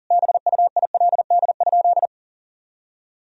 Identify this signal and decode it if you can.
Morse code, heard as a keyed tone, about 40 wpm, 700 Hz. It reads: BUILD?
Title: Sentences from Top 1000 Words - Review Part 3 - 40wpm